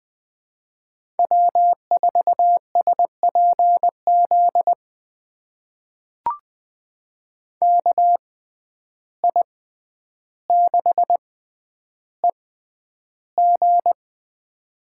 Morse code, heard as a keyed tone, about 20 wpm, spaced as for 10 wpm, 700 Hz. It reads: W4SPZ KI6EG
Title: USA Call Signs - 5 Characters - 6x Character Spacing - 20wpm